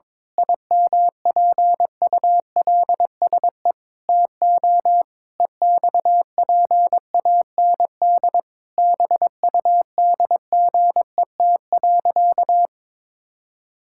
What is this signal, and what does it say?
Morse code, heard as a keyed tone, about 22 wpm, 700 Hz. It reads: IMPULSE TO EXPAND BUDGET.
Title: Pangram Challenge v2 - 22wpm